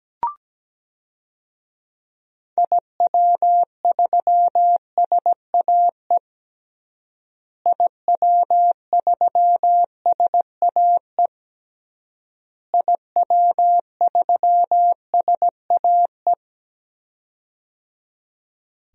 Morse code, heard as a keyed tone, about 17 wpm, 700 Hz. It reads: IW3SAE IW3SAE IW3SAE
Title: DXCC-100 Call Signs - 6 Characters - Repeated 3x - 17wpm